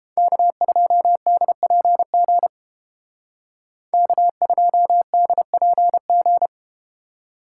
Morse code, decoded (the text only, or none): K2DPG K2DPG